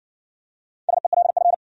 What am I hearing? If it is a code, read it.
Morse code, heard as a keyed tone, about 57 wpm, 700 Hz. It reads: SELF